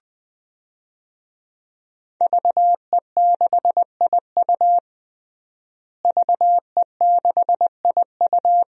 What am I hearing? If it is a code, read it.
Morse code, heard as a keyed tone, about 20 wpm, 700 Hz. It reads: VE6IU VE6IU
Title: DXCC-100 Call Signs - 5 Characters - Repeated 3x - 20wpm